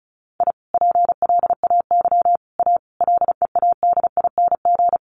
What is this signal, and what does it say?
Morse code, heard as a keyed tone, about 34 wpm, 700 Hz. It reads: I PLAY A LEADING
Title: Sentences from Top 1100 Words 35wpm